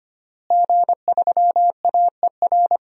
Morse code, heard as a keyed tone, about 25 wpm, 700 Hz. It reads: G3AER
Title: DXCC-100 Call Signs - 5 Characters - Repeated 3x - 25wpm